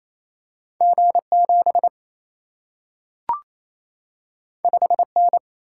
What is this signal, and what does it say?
Morse code, heard as a keyed tone, about 28 wpm, 700 Hz. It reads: G 7 5 N
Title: Single-Letter Single-Number 28wpm